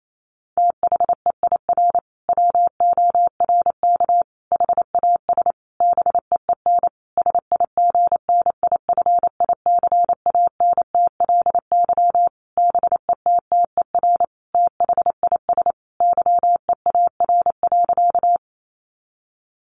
Morse code, heard as a keyed tone, about 28 wpm, 700 Hz. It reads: THEIR WORK HAS BEEN SIGNIFICANTLY BETTER THIS YEAR.
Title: Sentences from Top 1500 Words 28wpm